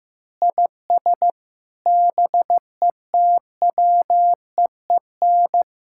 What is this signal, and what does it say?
Morse code, heard as a keyed tone, about 15 wpm, 700 Hz. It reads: IS BETWEEN
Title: Sentences from Top 1600 Words 15wpm